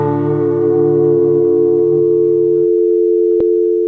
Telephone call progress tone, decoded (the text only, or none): dial tone